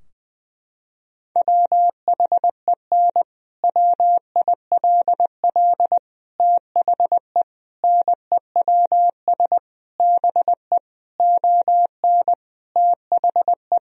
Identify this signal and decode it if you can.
Morse code, heard as a keyed tone, about 20 wpm, 700 Hz. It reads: WHEN WILL THE NEWS BE ON THE
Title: Sentences from Top 1000 Words - Review Part 3 - 20wpm